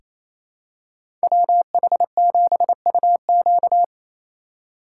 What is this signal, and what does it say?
Morse code, heard as a keyed tone, about 28 wpm, 700 Hz. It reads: WH7UQ
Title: USA Call Signs - 5 Characters - 2x Character Spacing - 28wpm